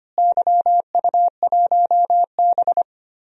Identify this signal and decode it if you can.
Morse code, heard as a keyed tone, about 25 wpm, 700 Hz. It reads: YU1B
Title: DXCC-100 Call Signs - 4 Characters - Repeated 3x - 25wpm